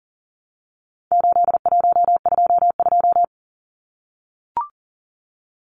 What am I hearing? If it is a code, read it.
Morse code, heard as a keyed tone, about 40 wpm, 700 Hz. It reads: 8122
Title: Numbers - 4-Digit 40wpm